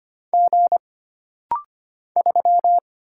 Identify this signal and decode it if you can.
Morse code, heard as a keyed tone, about 25 wpm, 700 Hz. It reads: G 3
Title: Single Letter-Number Alt - Rapid-Fire 25wpm